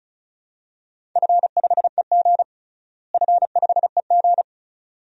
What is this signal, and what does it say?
Morse code, heard as a keyed tone, about 36 wpm, 700 Hz. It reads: F5EG F5EG
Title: DXCC-100 Call Signs - 4 Characters - Repeated 2x - 35wpm